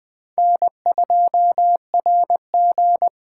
Morse code, heard as a keyed tone, about 21 wpm, 700 Hz. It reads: N2RG
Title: USA Call Signs - 8x Character Spacing - 20wpm